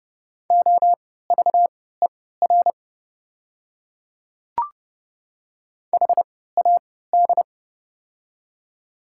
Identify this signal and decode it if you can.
Morse code, heard as a keyed tone, about 30 wpm, spaced as for 17 wpm, 690 Hz. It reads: OVER HAD
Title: 3x Character Spacing - Top 500 Words - 30wpm